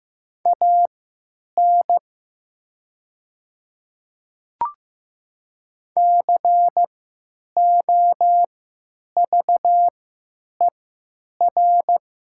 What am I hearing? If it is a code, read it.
Morse code, heard as a keyed tone, about 15 wpm, spaced as for 9 wpm, 700 Hz. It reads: AN COVER
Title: 3x Character Spacing - Top 300 Words - 15wpm